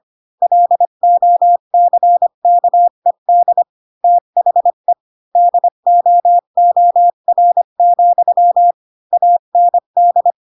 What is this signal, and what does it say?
Morse code, heard as a keyed tone, about 25 wpm, 700 Hz. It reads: LOCKED THE DOOR, AND
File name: Pinocchio / Pinocchio ch4 20 25 ik1hge